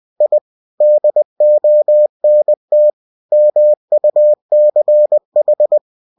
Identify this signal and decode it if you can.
Morse code, heard as a keyed tone, about 20 wpm, 600 Hz. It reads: I DONT MUCH